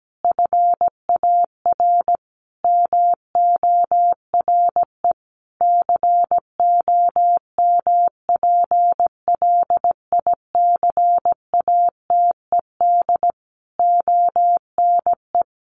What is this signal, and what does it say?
Morse code, heard as a keyed tone, about 17 wpm, 700 Hz. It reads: FAR MORE COMPLICATED ONE
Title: Sentences from Top 1200 Words 17wpm